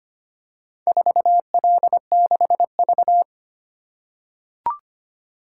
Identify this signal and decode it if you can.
Morse code, heard as a keyed tone, about 25 wpm, 700 Hz. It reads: 4L6V
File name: practice / 25wpm / Four Letter-Number Combinations - 25wpm